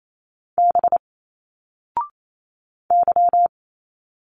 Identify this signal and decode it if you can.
Morse code, heard as a keyed tone, about 28 wpm, 700 Hz. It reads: B Y
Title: Single Letters 28wpm